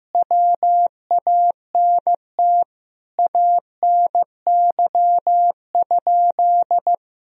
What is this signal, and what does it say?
Morse code, heard as a keyed tone, about 16 wpm, 700 Hz. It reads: WANT ANY?
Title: Sentences from Top 300 Words and Abbreviations 15wpm